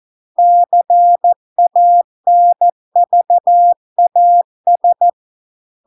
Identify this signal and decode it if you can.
Morse code, heard as a keyed tone, about 14 wpm, 700 Hz. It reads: CANVAS